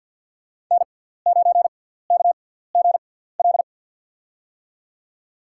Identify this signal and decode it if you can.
Morse code, heard as a keyed tone, about 51 wpm, spaced as for 17 wpm, 700 Hz. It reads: N9KGL
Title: USA Call Signs - 6x Character Spacing - 50wpm